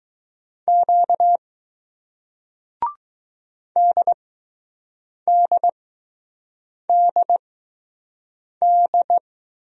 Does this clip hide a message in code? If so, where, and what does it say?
Yes: Morse code from 0.7 to 9.2 s, heard as a keyed tone, about 20 wpm, spaced as for 6 wpm, 700 Hz. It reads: Q DDDD